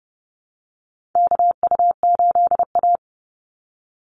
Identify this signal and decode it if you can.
Morse code, heard as a keyed tone, about 30 wpm, 700 Hz. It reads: KU8A
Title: USA Call Signs 30wpm